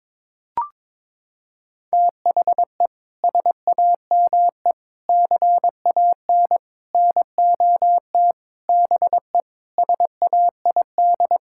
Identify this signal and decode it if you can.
Morse code, heard as a keyed tone, about 22 wpm, 700 Hz. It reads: THE SAME CAN NOT BE SAID
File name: practice / 22wpm / Sets of 7 Words - 22wpm